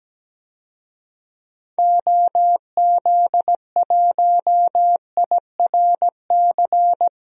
Morse code, heard as a keyed tone, about 17 wpm, 700 Hz. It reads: OZ1IRC
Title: DXCC-100 Call Signs - 6 Characters - 17wpm